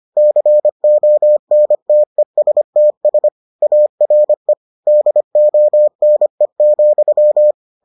Morse code, heard as a keyed tone, about 25 wpm, 600 Hz. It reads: CONTESTS ARE DONE,